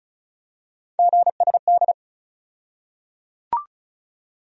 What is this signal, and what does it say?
Morse code, heard as a keyed tone, about 36 wpm, 700 Hz. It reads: GSD
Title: USA Call Sign Postfixes - 3 Letters 35wpm